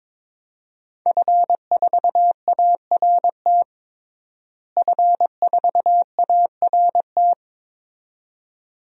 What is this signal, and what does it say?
Morse code, heard as a keyed tone, about 22 wpm, 700 Hz. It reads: F4ART F4ART
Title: DXCC-100 Call Signs - 5 Characters - Repeated 2x - 22wpm